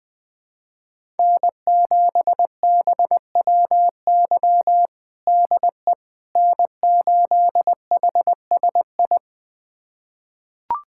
Morse code, heard as a keyed tone, about 21 wpm, 700 Hz. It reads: N7BWY DE N8HSI
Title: USA Callsign DE Callsign - 5 Characters - 20wpm